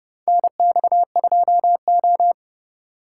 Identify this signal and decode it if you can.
Morse code, heard as a keyed tone, about 30 wpm, 700 Hz. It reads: NX2O
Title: USA Call Signs - 4 Characters - 2x Character Spacing - 30wpm